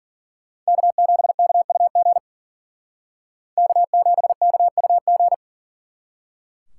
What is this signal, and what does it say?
Morse code, heard as a keyed tone, about 50 wpm, 700 Hz. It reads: K7KUG K7KUG
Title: Speed Racing - USA 1x3 Call Signs 40wpm